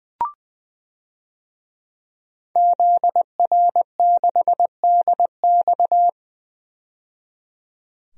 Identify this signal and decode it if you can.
Morse code, heard as a keyed tone, about 20 wpm, 700 Hz. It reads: ZR6DX
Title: DXCC-100 Call Signs - 5 Characters - 20wpm